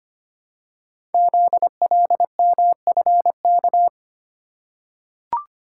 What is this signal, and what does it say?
Morse code, heard as a keyed tone, about 25 wpm, 700 Hz. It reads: ZLMFK
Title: Code Groups - 5 Letters - 25wpm